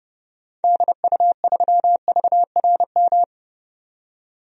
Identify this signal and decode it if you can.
Morse code, heard as a keyed tone, about 31 wpm, 700 Hz. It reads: DU3VRM
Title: DXCC-100 Call Signs - 6 Characters - 30wpm